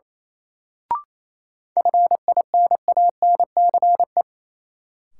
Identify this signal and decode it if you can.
Morse code, heard as a keyed tone, about 27 wpm, 700 Hz. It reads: FINANCE
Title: Sentences from Top 1500 Words - New Word Drill - 28wpm